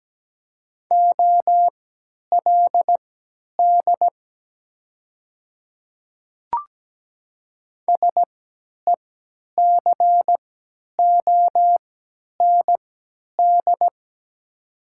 Morse code, heard as a keyed tone, about 17 wpm, spaced as for 10 wpm, 700 Hz. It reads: OLD SECOND
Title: 3x Character Spacing - Top 300 Words - 17wpm